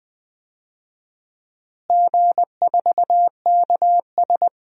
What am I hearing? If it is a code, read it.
Morse code, heard as a keyed tone, about 21 wpm, 700 Hz. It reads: G4KS
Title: DXCC-100 Call Signs - 4 Characters - Repeated 2x - 20wpm